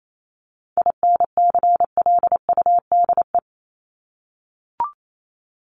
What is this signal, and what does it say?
Morse code, heard as a keyed tone, about 28 wpm, 700 Hz. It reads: INCLUDE
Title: ED Suffix - Top 100 Words with Base Words 28wpm